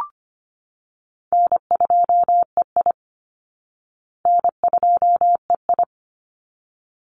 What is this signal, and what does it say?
Morse code, heard as a keyed tone, about 25 wpm, 700 Hz. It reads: N2EI N2EI
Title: Call Signs Repeated 3x - 25wpm